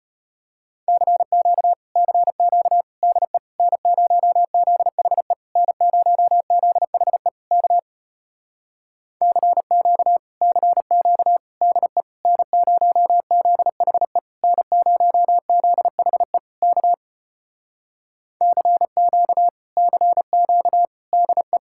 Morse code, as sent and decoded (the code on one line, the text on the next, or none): -.-. --.- / -.-. --.- / -.. / -. ----- --.. .... / -. ----- --.. .... / -.- / -.-. --.- / -.-. --.- / -.. / -. ----- --.. .... / -. ----- --.. .... / -.- / -.-. --.- / -.-. --.- / -.. .
CQ CQ DE N0ZHE N0ZHE K CQ CQ DE N0ZHE N0ZHE K CQ CQ DE